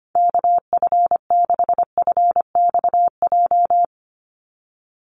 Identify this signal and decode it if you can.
Morse code, heard as a keyed tone, about 25 wpm, 700 Hz. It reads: KF6FXJ
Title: USA Call Signs 25wpm